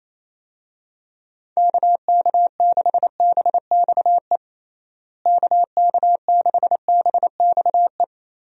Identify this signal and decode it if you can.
Morse code, heard as a keyed tone, about 28 wpm, 700 Hz. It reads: KK6BXE KK6BXE
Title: USA Call Signs - 6 Characters - Repeated 3x - 28wpm